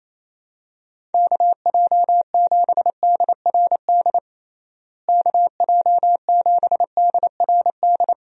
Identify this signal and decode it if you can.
Morse code, heard as a keyed tone, about 28 wpm, 700 Hz. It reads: KJ7DRD KJ7DRD
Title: USA Call Signs - 6 Characters - Repeated 2x - 28wpm